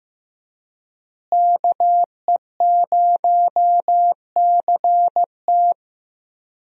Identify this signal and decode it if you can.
Morse code, heard as a keyed tone, about 16 wpm, 700 Hz. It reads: KE0CT
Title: USA Call Signs - 5 Characters - Repeated 3x - 15wpm